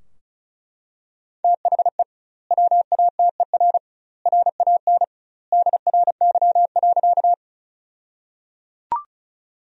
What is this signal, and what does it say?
Morse code, heard as a keyed tone, about 34 wpm, 700 Hz. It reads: THE WATER RAN DRY.
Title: 2x Word-Spacing - Sentences from Top 500 Words - 35wpm